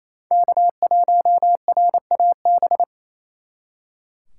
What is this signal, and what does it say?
Morse code, heard as a keyed tone, about 36 wpm, 700 Hz. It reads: K1RAB